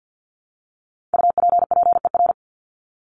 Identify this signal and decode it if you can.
Morse code, heard as a keyed tone, about 50 wpm, 700 Hz. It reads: UPPER
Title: Top 1751-2000 Words 50wpm